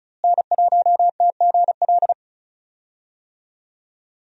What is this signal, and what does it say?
Morse code, heard as a keyed tone, about 36 wpm, 700 Hz. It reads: N1TGL
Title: USA Call Signs - 5 Characters - Repeated 2x - 35wpm